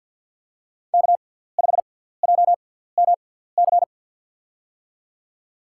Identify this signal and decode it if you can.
Morse code, heard as a keyed tone, about 53 wpm, spaced as for 17 wpm, 700 Hz. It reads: K5JMC
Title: USA Call Signs - 5 Characters - 6x Character Spacing - 50wpm